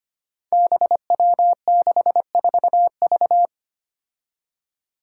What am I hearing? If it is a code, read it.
Morse code, heard as a keyed tone, about 25 wpm, 700 Hz. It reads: BW64V
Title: Five Challenging Letter-Number Combinations 25wpm